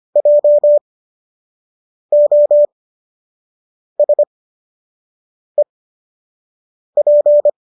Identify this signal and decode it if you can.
Morse code, heard as a keyed tone, about 25 wpm, spaced as for 6 wpm, 590 Hz. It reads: JOSEP